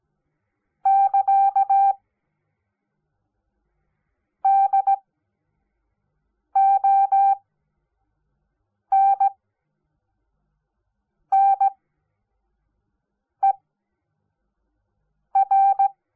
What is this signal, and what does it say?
Morse code, heard as a keyed tone, about 17 wpm, spaced as for 4 wpm, 780 Hz. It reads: <KA> DONNER
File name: Onlinelehrgang / morsen / DJ4UF Morsen05